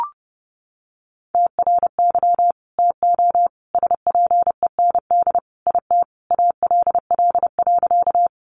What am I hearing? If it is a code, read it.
Morse code, heard as a keyed tone, about 30 wpm, 700 Hz. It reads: TRY TO SPEND IT ALL.
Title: Sentences from Top 800 Words 30wpm